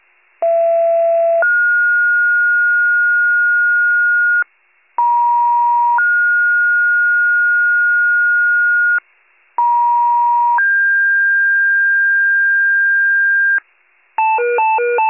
Two-tone page, heard as a one-second tone, then a three-second tone, about 670 then 1430 Hz, about 950 then 1430 Hz, about 950 then 1600 Hz, with noise. It is heard from 0.4 to 13.6 s.
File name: pager / 39 FIRE / wcalert